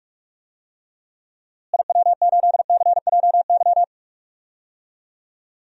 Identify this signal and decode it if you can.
Morse code, heard as a keyed tone, about 46 wpm, 700 Hz. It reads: IW8CJY
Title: DXCC-100 Call Signs - 6 Characters - Repeated 2x - 45wpm